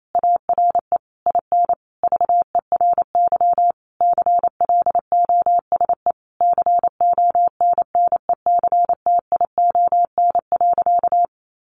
Morse code, heard as a keyed tone, about 28 wpm, 700 Hz. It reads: ARE IN VERY CLOSE CONNECTION.
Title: Sentences from Top 1300 Words 28wpm